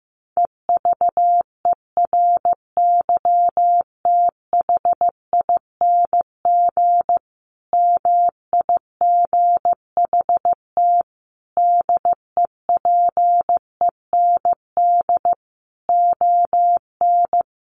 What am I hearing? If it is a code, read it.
Morse code, heard as a keyed tone, about 15 wpm, 700 Hz. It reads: EVERYTHING MIGHT DEPEND ON
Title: Sentences from Top 1300 Words 15wpm